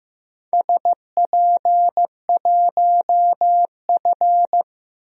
Morse code, heard as a keyed tone, about 15 wpm, 700 Hz. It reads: SP1F
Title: DXCC-100 Call Signs - 4 Characters - Repeated 2x - 15wpm